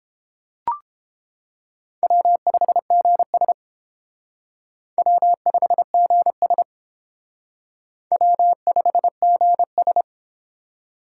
Morse code, heard as a keyed tone, about 31 wpm, 700 Hz. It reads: W5GS W5GS W5GS